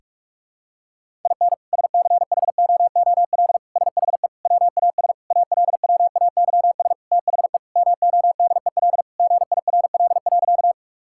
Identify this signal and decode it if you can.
Morse code, heard as a keyed tone, about 46 wpm, 700 Hz. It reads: IN SCHOOL SHE WAS ALWAYS THE MODEL GIRL.
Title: Sentences from Top 1000 Words - Review Part 5 - 45wpm